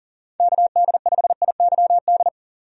Morse code, heard as a keyed tone, about 40 wpm, 700 Hz. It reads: KD5IYD
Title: USA Call Signs - 6 Characters - 40wpm